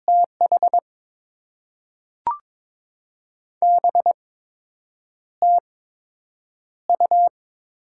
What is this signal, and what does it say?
Morse code, heard as a keyed tone, about 22 wpm, spaced as for 8 wpm, 700 Hz. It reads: TH BTU
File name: Top 20 QSO Elements - 8x Character Spacing - 22wpm